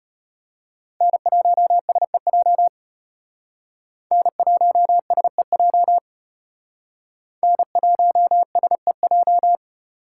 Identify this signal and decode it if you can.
Morse code, heard as a keyed tone, about 32 wpm, 700 Hz. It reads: N1SEJ N1SEJ N1SEJ